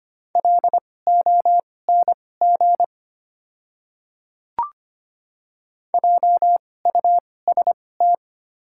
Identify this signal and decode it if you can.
Morse code, heard as a keyed tone, about 25 wpm, spaced as for 18 wpm, 690 Hz. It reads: LONG JUST
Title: 2x Character Spacing - Top 200 Words - 25wpm